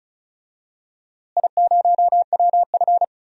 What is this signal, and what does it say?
Morse code, heard as a keyed tone, about 36 wpm, 700 Hz. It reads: I0WF
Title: DXCC-100 Call Signs - 4 Characters - Repeated 2x - 35wpm